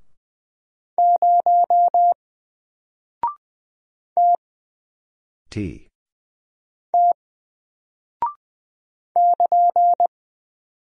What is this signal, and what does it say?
Morse code, heard as a keyed tone, about 20 wpm, 700 Hz. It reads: 0 T T <KN>
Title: Single Character (Letter:Number:Punctuation:Prosign) 20wpm